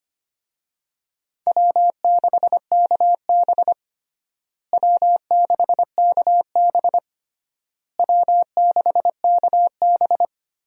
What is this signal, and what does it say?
Morse code, heard as a keyed tone, about 25 wpm, 700 Hz. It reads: W6KB W6KB W6KB
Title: USA Call Signs - 4 Characters - Repeated 3x - 25wpm